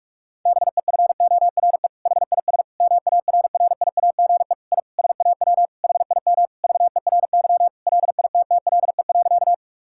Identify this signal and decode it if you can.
Morse code, heard as a keyed tone, about 46 wpm, 700 Hz. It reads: BEFORE HIS MARRIAGE I SAW HIM VERY LITTLE.
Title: Sentences from Top 1000 Words - Review Part 5 - 45wpm